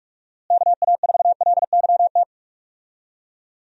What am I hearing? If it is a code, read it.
Morse code, heard as a keyed tone, about 41 wpm, 700 Hz. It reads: KA4LYT
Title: USA Call Signs - 6 Characters - Repeated 3x - 45wpm